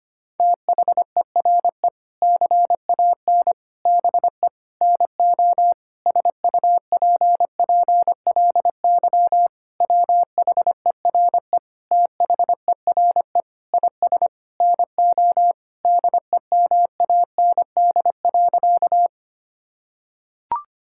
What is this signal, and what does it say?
Morse code, heard as a keyed tone, about 25 wpm, 700 Hz. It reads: THERE CAN BE NO SUPPLY WHERE THERE IS NO DEMAND.
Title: Sentences from Top 1000 Words - Review Part 5 - 25wpm